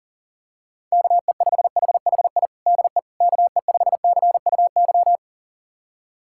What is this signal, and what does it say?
Morse code, heard as a keyed tone, about 40 wpm, 700 Hz. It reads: KE5HHI DE KE5CUY
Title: USA Callsign DE Callsign - 6 Characters - 40wpm